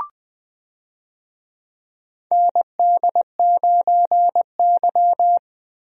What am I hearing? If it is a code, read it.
Morse code, heard as a keyed tone, about 21 wpm, 700 Hz. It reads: ND9Y
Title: USA Call Signs - 4 Characters - Repeated 3x - 20wpm